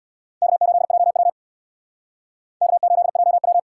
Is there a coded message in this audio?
Yes: Morse code from 0.4 to 3.6 s, heard as a keyed tone, about 63 wpm, 700 Hz. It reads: K8JW K8JW